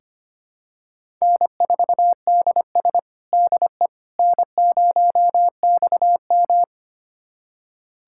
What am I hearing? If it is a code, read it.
Morse code, heard as a keyed tone, about 25 wpm, 700 Hz. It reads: N4DS DE N0XM
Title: USA Callsign DE Callsign - 4 Characters - 25wpm